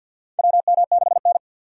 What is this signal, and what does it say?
Morse code, heard as a keyed tone, about 52 wpm, 700 Hz. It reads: WM6N
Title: USA Call Signs - 4 Characters - 3x Character Spacing - 50wpm